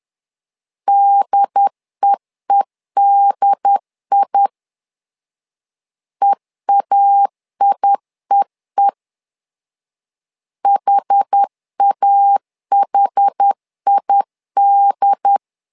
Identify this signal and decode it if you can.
Morse code, heard as a keyed tone, about 10 wpm, 800 Hz. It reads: DEEDI EAIEE HAHID
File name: LEZIONI / 2 UHDN / 2 UHD 10wpm 1826